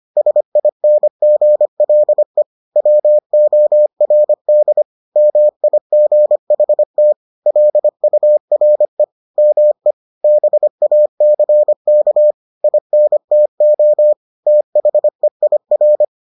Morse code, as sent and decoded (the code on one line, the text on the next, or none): ... .. -. --. .-.. / .-- --- .-. -.. / -- .. --. .... - / .-.. ..- .-. / -- . / -... .- -.-. -.- / .. -. - --- / - .... .. .-.
SINGLE WORD MIGHT LURE ME BACK INTO THEIR